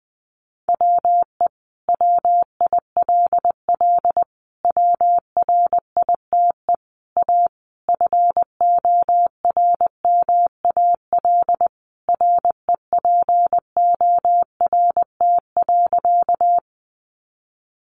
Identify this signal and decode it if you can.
Morse code, heard as a keyed tone, about 20 wpm, 700 Hz. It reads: WE WILL WRITE A FORMAL REPORT.
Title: Sentences from Top 1300 Words 20wpm